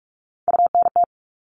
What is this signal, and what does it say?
Morse code, heard as a keyed tone, about 44 wpm, 700 Hz. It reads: UNT
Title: All Cut Numbers - 3-Digit 45wpm